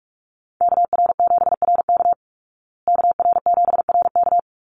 Morse code, heard as a keyed tone, about 44 wpm, 700 Hz. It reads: KR7RK KR7RK
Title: Call Signs Repeated 2x - 45wpm